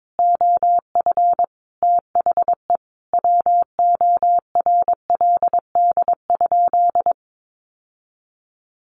Morse code, heard as a keyed tone, about 22 wpm, 700 Hz. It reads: OF THE WORLD?
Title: Sentences from Top 900 Words 22wpm